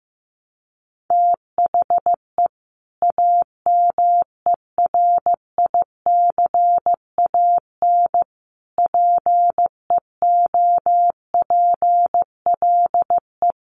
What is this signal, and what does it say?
Morse code, heard as a keyed tone, about 15 wpm, 700 Hz. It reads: THE AMERICAN PEOPLE
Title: Sets of 3 Words 15wpm